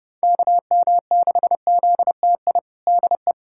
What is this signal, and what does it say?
Morse code, heard as a keyed tone, about 31 wpm, 700 Hz. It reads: KM6ZTI DE
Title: USA Callsign DE Callsign - 6 Characters - 30wpm